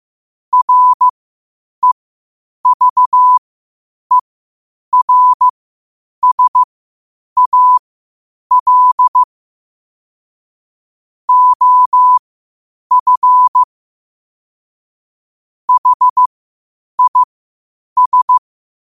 Morse code, heard as a keyed tone, about 15 wpm, spaced as for 8 wpm, 1000 Hz. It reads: REVERSAL OF HIS
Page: 8 wpm morse code quotes